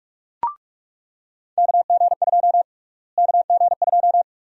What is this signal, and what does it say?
Morse code, heard as a keyed tone, about 45 wpm, 700 Hz. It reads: KG2 KG2
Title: USA Call Sign Prefixes - Repeated 2x - 45wpm